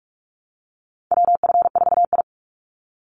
Morse code, heard as a keyed tone, about 46 wpm, 700 Hz. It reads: WF4I
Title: NA SOTA Chaser Callsigns 45wpm